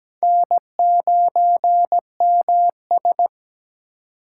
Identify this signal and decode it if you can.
Morse code, heard as a keyed tone, about 17 wpm, 700 Hz. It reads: N9MS
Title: K1USN SST QSO Practice - 17wpm